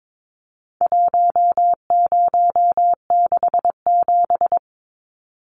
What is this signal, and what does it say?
Morse code, heard as a keyed tone, about 22 wpm, 700 Hz. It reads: 1067
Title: Numbers - 4-Digit 22wpm